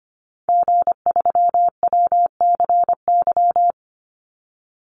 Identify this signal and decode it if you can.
Morse code, heard as a keyed tone, about 25 wpm, 700 Hz. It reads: G3WCY
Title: Call Signs Repeated 3x - 25wpm